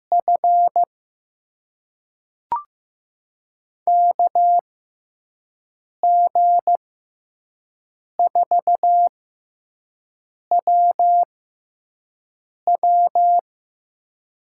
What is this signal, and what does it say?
Morse code, heard as a keyed tone, about 15 wpm, spaced as for 5 wpm, 700 Hz. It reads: F KG4WW
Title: USA Call Signs - 6 Characters - 6x Character Spacing - 15wpm